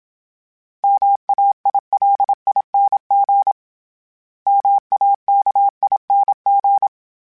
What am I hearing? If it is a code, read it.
Morse code, heard as a keyed tone, about 27 wpm, 800 Hz. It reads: MAILING MAKING
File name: Special INGs-mp3s / Special INGs-mp3s / Sngl Ing @25